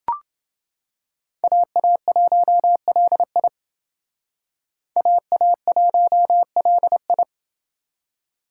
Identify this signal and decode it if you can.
Morse code, heard as a keyed tone, about 28 wpm, 700 Hz. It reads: AA1LI AA1LI